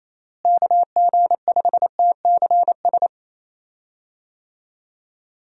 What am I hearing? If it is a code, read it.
Morse code, heard as a keyed tone, about 28 wpm, 700 Hz. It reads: KG5TCS
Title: USA Call Signs - 6 Characters - Repeated 2x - 28wpm